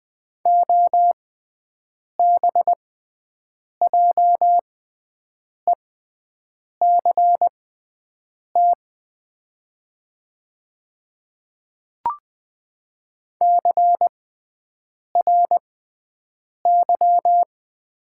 Morse code, heard as a keyed tone, about 20 wpm, spaced as for 7 wpm, 700 Hz. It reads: OBJECT CRY
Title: 6x Character Spacing - Top 500 Words - 20wpm